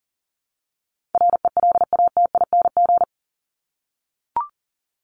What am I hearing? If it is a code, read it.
Morse code, heard as a keyed tone, about 40 wpm, 700 Hz. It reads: RELATING
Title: ING Suffix - Top 100 Words with Base Words 40wpm